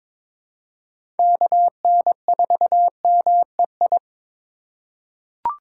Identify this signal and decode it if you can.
Morse code, heard as a keyed tone, about 22 wpm, 700 Hz. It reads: KN4MEI
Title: USA Call Signs - 2x Character Spacing - 22wpm